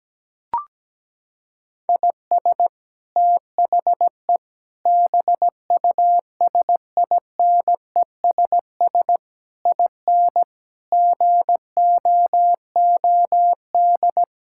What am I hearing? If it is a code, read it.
Morse code, heard as a keyed tone, about 17 wpm, 700 Hz. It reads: IS THE BUSINESS IN GOOD
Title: Sentences from Top 1000 Words - Review Part 5 - 17wpm